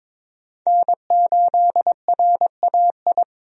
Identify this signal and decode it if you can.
Morse code, heard as a keyed tone, about 22 wpm, 700 Hz. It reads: N8RAI
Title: USA Call Signs - 5 Characters - 22wpm